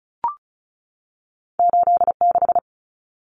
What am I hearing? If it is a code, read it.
Morse code, heard as a keyed tone, about 36 wpm, 700 Hz. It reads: 86
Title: Common Cut Numbers - 2-Digit 35wpm